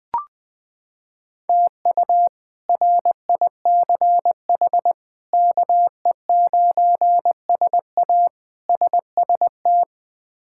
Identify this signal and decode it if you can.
Morse code, heard as a keyed tone, about 20 wpm, 700 Hz. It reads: TU RICH KE9SA SST
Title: K1USN SST QSO Practice - 20wpm